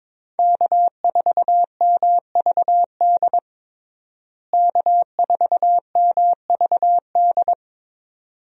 Morse code, heard as a keyed tone, about 22 wpm, 700 Hz. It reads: K4MVD K4MVD
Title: USA Call Signs - 5 Characters - Repeated 2x - 22wpm